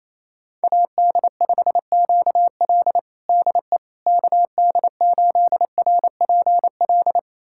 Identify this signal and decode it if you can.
Morse code, heard as a keyed tone, about 28 wpm, 700 Hz. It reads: AD5QL DE KD8RPL
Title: USA Callsign DE Callsign 28wpm